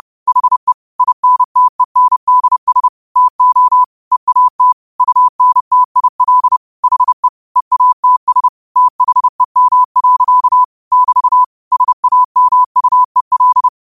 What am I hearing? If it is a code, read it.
Morse code, heard as a keyed tone, about 30 wpm, 1000 Hz. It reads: HE INTENDS TO EAT UNTIL HE EATS THEM. <BT> SAMUEL
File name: qotdfiles / 30wpm